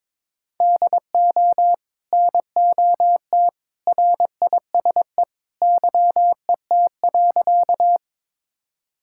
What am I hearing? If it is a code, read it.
Morse code, heard as a keyed tone, about 22 wpm, 700 Hz. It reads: DO NOT RISE YET.
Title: Sentences from Top 1000 Words - Review Part 6 - 22wpm